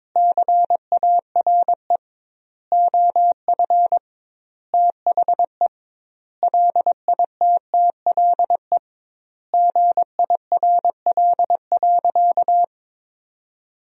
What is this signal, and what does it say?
Morse code, heard as a keyed tone, about 22 wpm, 700 Hz. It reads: CARE OF THE LITTLE GIRL.
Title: 2x Word-Spacing - Sentences from Top 400 Words - 22wpm